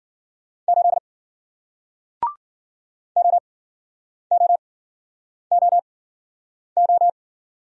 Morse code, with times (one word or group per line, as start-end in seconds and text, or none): 0.7-1.0 8
3.2-7.1 OOOO